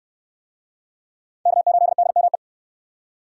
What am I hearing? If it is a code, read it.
Morse code, heard as a keyed tone, about 69 wpm, 700 Hz. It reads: K8DME